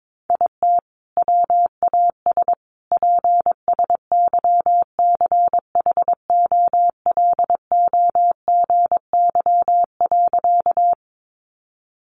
Morse code, heard as a keyed tone, about 22 wpm, 700 Hz. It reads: IT WAS PSYCHOLOGY.
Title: Sentences from Top 1400 Words 22wpm